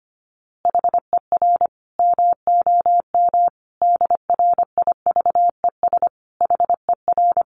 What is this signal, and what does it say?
Morse code, heard as a keyed tone, about 25 wpm, 700 Hz. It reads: HER MOM DRIVES HER
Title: Easy Phrases 25wpm